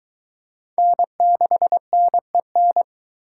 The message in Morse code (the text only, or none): N6NEN